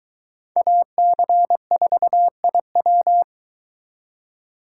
Morse code, heard as a keyed tone, about 22 wpm, 700 Hz. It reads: AC4IW